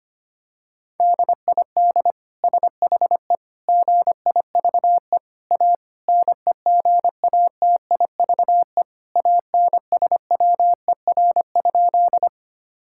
Morse code, heard as a keyed tone, about 25 wpm, 700 Hz. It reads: DID SHE GIVE A NEGATIVE ANSWER?
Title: Sentences from Top 1000 Words - Review Part 6 - 25wpm